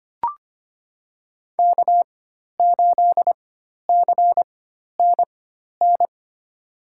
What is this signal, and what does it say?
Morse code, heard as a keyed tone, about 25 wpm, spaced as for 12 wpm, 700 Hz. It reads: K8CNN
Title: USA Call Signs - 5 Characters - 4x Character Spacing - 25wpm